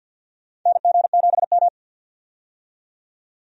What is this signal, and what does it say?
Morse code, heard as a keyed tone, about 50 wpm, 700 Hz. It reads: NG7M
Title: USA Call Signs - 4 Characters - 50wpm